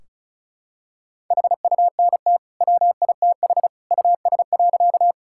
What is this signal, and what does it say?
Morse code, heard as a keyed tone, about 36 wpm, 700 Hz. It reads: HUNT WITH US.
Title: Sentences from Top 1600 Words 35wpm